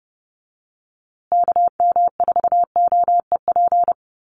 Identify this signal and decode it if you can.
Morse code, heard as a keyed tone, about 30 wpm, 700 Hz. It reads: KM4OEP
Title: USA Call Signs Repeated 3x 30wpm